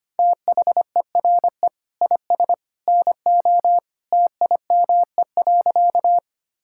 Morse code, heard as a keyed tone, about 25 wpm, 700 Hz. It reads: THERE IS NO TIME.
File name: practice / 25wpm / Sentences with 2-Word Phrases - 25wpm